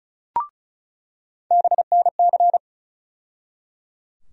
Morse code, heard as a keyed tone, about 34 wpm, 700 Hz. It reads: BNC
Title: USA Call Sign Postfixes - 3 Letters 35wpm